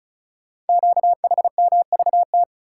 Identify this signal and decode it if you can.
Morse code, heard as a keyed tone, about 36 wpm, 690 Hz. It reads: QHMVT
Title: Code Groups - 5 Letters - 35wpm